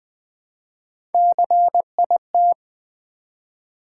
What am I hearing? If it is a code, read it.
Morse code, heard as a keyed tone, about 20 wpm, 700 Hz. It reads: CIT
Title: Build A Word - 1 Letter At A Time - 4-Letter Words - 20wpm